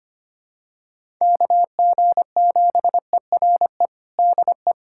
Morse code, heard as a keyed tone, about 25 wpm, 700 Hz. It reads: KG7ERE DE
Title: USA Callsign DE Callsign 25wpm